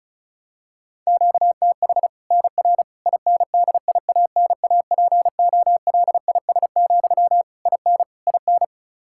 Morse code, heard as a keyed tone, about 36 wpm, 700 Hz. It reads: QTH NR INDIANAPOLIS, IN IN
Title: Practice QSO - Exchange 1 - Beginning - 35wpm